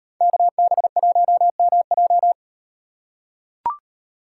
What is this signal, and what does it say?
Morse code, heard as a keyed tone, about 38 wpm, 700 Hz. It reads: KB1MJ